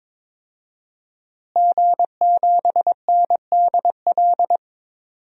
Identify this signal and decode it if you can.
Morse code, heard as a keyed tone, about 22 wpm, 700 Hz. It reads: G7NDL
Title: DXCC-100 Call Signs - 5 Characters - Repeated 3x - 22wpm